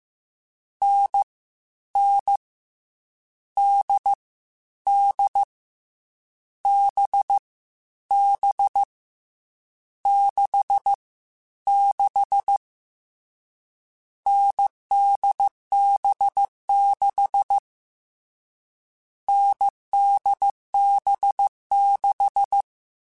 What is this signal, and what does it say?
Morse code, heard as a keyed tone, about 15 wpm, 800 Hz. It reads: N N D D B B 6 6 NDB6 NDB6